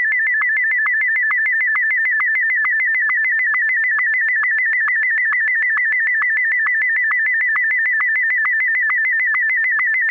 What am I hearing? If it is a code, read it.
SSTV (slow-scan television), Martin 1.